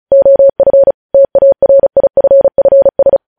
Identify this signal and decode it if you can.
Morse code, heard as a keyed tone, about 36 wpm, 550 Hz. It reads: OF TARIFFS
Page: News Headlines in Morse Code at 35 WPM